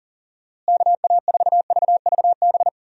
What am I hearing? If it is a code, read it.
Morse code, heard as a keyed tone, about 40 wpm, 700 Hz. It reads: KA4VVB
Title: USA Call Signs - 6 Characters - Repeated 2x - 40wpm